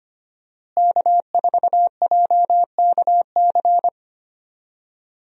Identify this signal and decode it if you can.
Morse code, heard as a keyed tone, about 25 wpm, 700 Hz. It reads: K4JKC
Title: USA Call Signs - 5 Characters - Repeated 3x - 25wpm